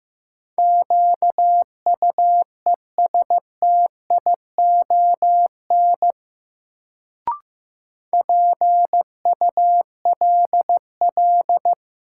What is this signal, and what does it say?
Morse code, heard as a keyed tone, about 15 wpm, 700 Hz. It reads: QUESTION PULL
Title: Sentences from Top 400 Words - New Word Drill - 15wpm